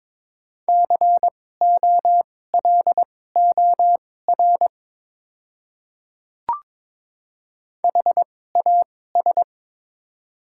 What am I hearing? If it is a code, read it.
Morse code, heard as a keyed tone, about 22 wpm, spaced as for 16 wpm, 700 Hz. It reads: COLOR HAS
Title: 2x Character Spacing - Top 300 Words - 22wpm